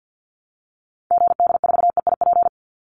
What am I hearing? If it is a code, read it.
Morse code, heard as a keyed tone, about 50 wpm, 700 Hz. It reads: GD4EIP
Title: Call Signs Repeated 3x - 50wpm